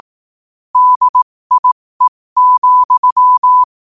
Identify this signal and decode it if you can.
Morse code, heard as a keyed tone, about 18 wpm, spaced as for 15 wpm, 1000 Hz. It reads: DIE,
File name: qotdfiles / 15wpm